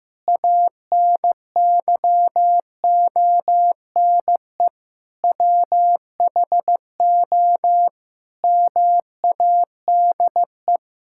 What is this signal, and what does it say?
Morse code, heard as a keyed tone, about 15 wpm, 700 Hz. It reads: ANYONE WHO MADE
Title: Sentences from Top 1000 Words - Review Part 6 - 15wpm